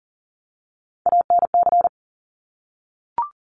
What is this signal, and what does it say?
Morse code, heard as a keyed tone, about 40 wpm, 700 Hz. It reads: ANC
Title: Top Three Letter Combinations in Words 40wpm